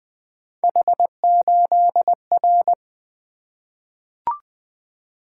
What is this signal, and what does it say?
Morse code, heard as a keyed tone, about 20 wpm, 700 Hz. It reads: H8R
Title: Three Letter-Number Combinations - 20wpm